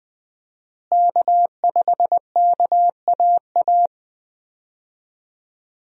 Morse code, heard as a keyed tone, about 20 wpm, 700 Hz. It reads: K5KAA
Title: USA Call Signs - 5 Characters - Repeated 2x - 20wpm